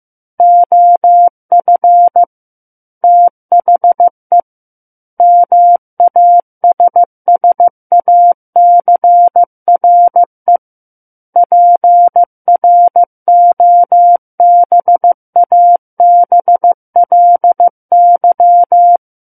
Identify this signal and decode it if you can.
Morse code, heard as a keyed tone, about 15 wpm, 700 Hz. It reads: OF THE MASSACRE PROBABLY
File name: Book One / 06-Chapter 6 - 15 WPM